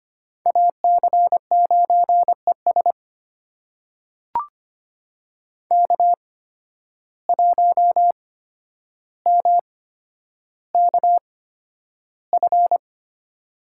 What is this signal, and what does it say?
Morse code, heard as a keyed tone, about 25 wpm, spaced as for 11 wpm, 700 Hz. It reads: AC9ES K1MKF
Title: USA Call Signs - 5 Characters - 8x Character Spacing - 25wpm